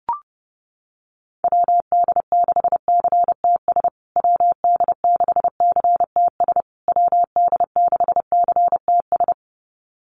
Morse code, heard as a keyed tone, about 30 wpm, 700 Hz. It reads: WD6CTS WD6CTS WD6CTS
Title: USA Call Signs Repeated 3x 30wpm